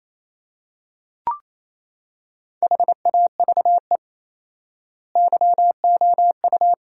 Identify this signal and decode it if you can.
Morse code, heard as a keyed tone, about 28 wpm, 700 Hz. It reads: HAVE YOU
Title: 4x Word-Spacing - Sentences from Top 300 Words - 28wpm